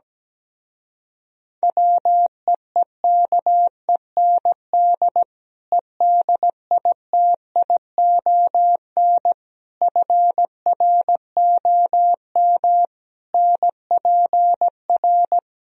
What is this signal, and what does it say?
Morse code, heard as a keyed tone, about 17 wpm, 700 Hz. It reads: WEEKEND EDITION FROM NPR
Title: Speed-Racing - Sets of 5 Words 17wpm